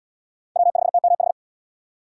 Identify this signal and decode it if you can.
Morse code, heard as a keyed tone, about 66 wpm, 700 Hz. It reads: W5TMH